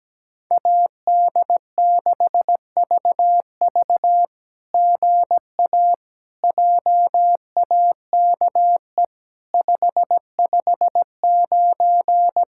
Morse code, heard as a keyed tone, about 17 wpm, 700 Hz. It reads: AD6VV GA JAKE 559
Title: SOTA QSO Practice 17wpm